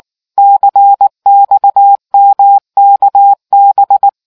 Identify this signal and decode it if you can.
Morse code, heard as a keyed tone, about 19 wpm, 800 Hz. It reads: CXMKB